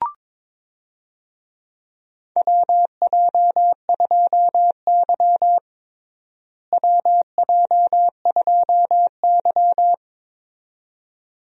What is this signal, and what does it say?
Morse code, heard as a keyed tone, about 22 wpm, 700 Hz. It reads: WJ2Y WJ2Y
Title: USA Call Signs - 4 Characters - Repeated 2x - 22wpm